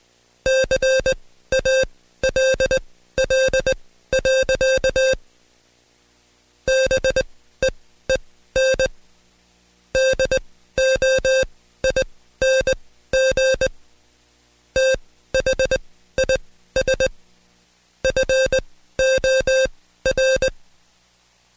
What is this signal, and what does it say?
Morse code, heard as a keyed tone, about 20 wpm, spaced as for 13 wpm, 530 Hz. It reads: CALL. BEEN DOING THIS FOR
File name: Code Quick / AudioDownld2 / Seg133-Stepper6 13WPM